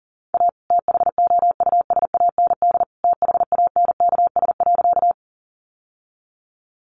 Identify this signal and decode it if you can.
Morse code, heard as a keyed tone, about 37 wpm, 700 Hz. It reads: A THOUSAND THANKS.
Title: Sentences from Top 600 Words 40wpm